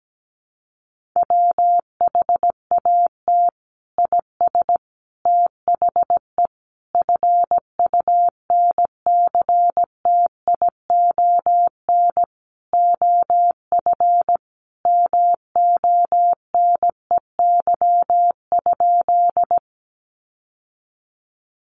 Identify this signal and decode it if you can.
Morse code, heard as a keyed tone, about 17 wpm, 700 Hz. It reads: WHAT IS THE FUNCTION OF MONEY?
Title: Sentences from Top 1100 Words 17wpm